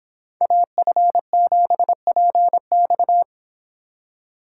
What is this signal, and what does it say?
Morse code, heard as a keyed tone, about 25 wpm, 700 Hz. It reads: AF7PX